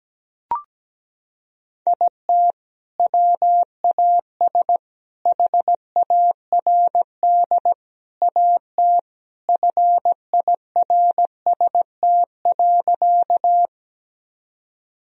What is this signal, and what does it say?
Morse code, heard as a keyed tone, about 17 wpm, 700 Hz. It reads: IT WAS HARD AT FIRST.
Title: Sentences from Top 1000 Words - Review Part 1 - 17wpm